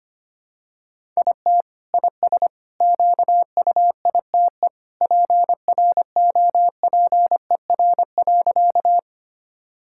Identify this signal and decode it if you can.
Morse code, heard as a keyed tone, about 25 wpm, 700 Hz. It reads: IT IS QUITE PROPER.
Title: Sentences from Top 1000 Words - Review Part 5 - 25wpm